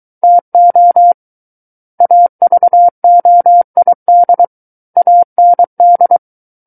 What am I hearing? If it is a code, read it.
Morse code, heard as a keyed tone, about 23 wpm, 700 Hz. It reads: TO AVOID AND
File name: Book One / 14-Chapter 14 - 23 WPM